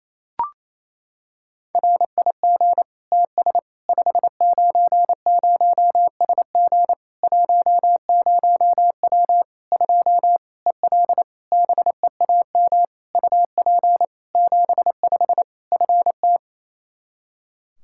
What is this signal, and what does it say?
Morse code, heard as a keyed tone, about 28 wpm, 700 Hz. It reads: RIG TS 590SG 10W 2 EL BEAM UP 75 FT
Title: Practice QSO - Exchange 2 - Info - 28wpm